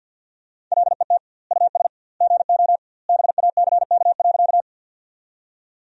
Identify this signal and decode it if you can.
Morse code, heard as a keyed tone, about 47 wpm, 700 Hz. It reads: LET US GO BACK.
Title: 2x Word-Spacing - Sentences from Top 300 Words - 50wpm